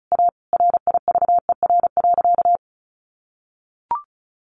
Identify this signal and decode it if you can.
Morse code, heard as a keyed tone, about 34 wpm, 700 Hz. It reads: A RIVER.
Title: Sentences from Top 1100 Words 35wpm